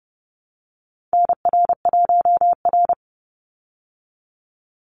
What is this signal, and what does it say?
Morse code, heard as a keyed tone, about 30 wpm, 700 Hz. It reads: NR1R
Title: Call Signs Repeated 3x - 30wpm